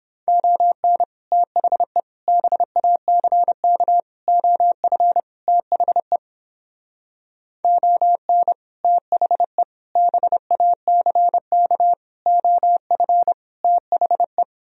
Morse code, heard as a keyed tone, about 29 wpm, 700 Hz. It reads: ON THE BACK OF THE ON THE BACK OF THE